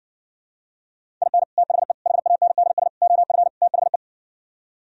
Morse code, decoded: IN THE HANDS OF THE

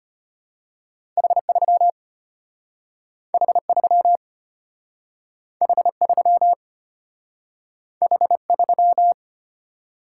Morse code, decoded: H3 H3 H3 H3

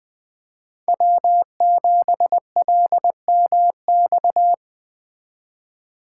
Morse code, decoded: W7LMX

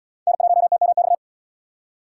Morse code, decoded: N1TMX

